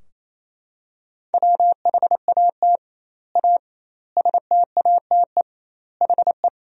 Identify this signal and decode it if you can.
Morse code, heard as a keyed tone, about 28 wpm, 700 Hz. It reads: WHAT A STATE HE
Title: 2x Word-Spacing - Sentences from Top 400 Words - 28wpm